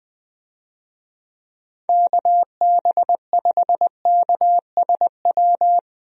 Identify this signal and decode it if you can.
Morse code, heard as a keyed tone, about 21 wpm, 700 Hz. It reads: KB5KSW
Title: USA Call Signs - 6 Characters - 20wpm